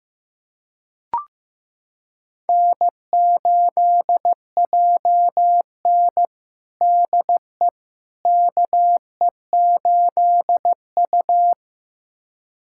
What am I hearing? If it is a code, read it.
Morse code, heard as a keyed tone, about 15 wpm, 700 Hz. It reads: N8JN DE KE8U
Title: USA Callsign DE Callsign - 4 Characters - 15wpm